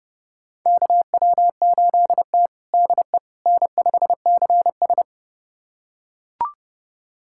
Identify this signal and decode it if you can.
Morse code, heard as a keyed tone, about 30 wpm, 700 Hz. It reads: KW8T DE N5CS
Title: USA Callsign DE Callsign - 4 Characters - 30wpm